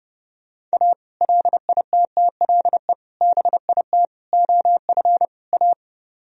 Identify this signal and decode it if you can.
Morse code, heard as a keyed tone, about 37 wpm, 700 Hz. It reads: A LITTLE BIT OF A
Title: Speed-Racing - Sets of 5 Words 30wpm